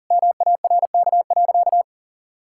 Morse code, heard as a keyed tone, about 40 wpm, 700 Hz. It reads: MARK.
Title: Sentences from Top 1000 Words - Review Part 6 - 40wpm